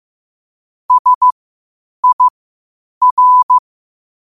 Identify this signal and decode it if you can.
Morse code, heard as a keyed tone, about 15 wpm, spaced as for 8 wpm, 1000 Hz. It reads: SIR
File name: qotdfiles / 8wpm